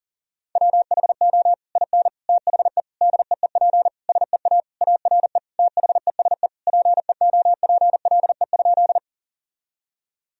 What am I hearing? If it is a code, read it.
Morse code, heard as a keyed tone, about 40 wpm, 700 Hz. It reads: WHO IN THE DEEP SEA ARE THESE PEOPLE?
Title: Sentences from Top 1000 Words - Review Part 3 - 40wpm